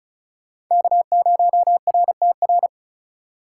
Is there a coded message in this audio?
Yes: Morse code from 0.7 to 2.7 s, heard as a keyed tone, about 44 wpm, 700 Hz. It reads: K0RTR